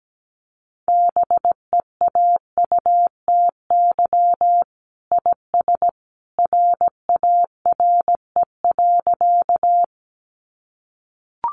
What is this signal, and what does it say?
Morse code, heard as a keyed tone, about 17 wpm, 700 Hz. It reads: BEAUTY IS RARE.
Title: Sentences from Top 1100 Words 17wpm